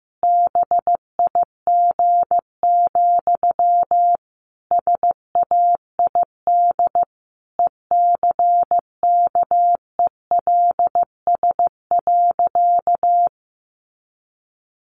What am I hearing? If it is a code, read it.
Morse code, heard as a keyed tone, about 14 wpm, 700 Hz. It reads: BIG, SAID ECKELS.
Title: A Sound Of Thunder 15wpm